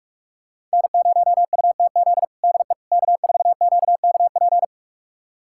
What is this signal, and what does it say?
Morse code, heard as a keyed tone, about 46 wpm, 700 Hz. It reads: N0UTZ DE K4QKP
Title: USA Callsign DE Callsign - 5 Characters - 45wpm